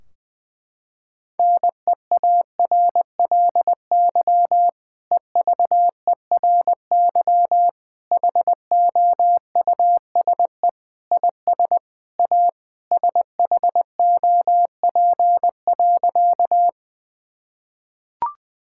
Morse code, heard as a keyed tone, about 20 wpm, 700 Hz. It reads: NEARLY EVERY HOUSE IS A SHOP.
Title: Sentences from Top 1600 Words 20wpm